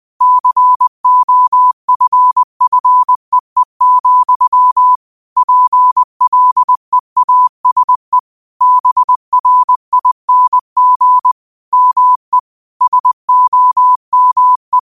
Morse code, heard as a keyed tone, about 20 wpm, 1000 Hz. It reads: COFFEE, PLEASE BRING ME SOME
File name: qotdfiles / 20wpm